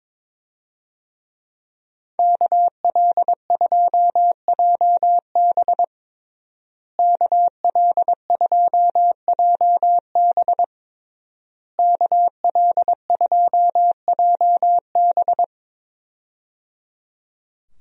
Morse code, heard as a keyed tone, about 22 wpm, 700 Hz. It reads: KL2JB KL2JB KL2JB
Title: USA Call Signs - 5 Characters - Repeated 3x - 22wpm